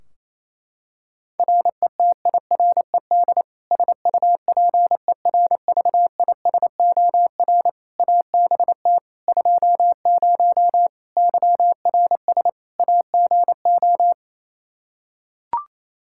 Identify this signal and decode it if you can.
Morse code, heard as a keyed tone, about 28 wpm, 700 Hz. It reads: RETIRED SUPERVISOR ABT 20 YRS AGO
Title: Practice QSO - Exchange 3 - More Info - 28wpm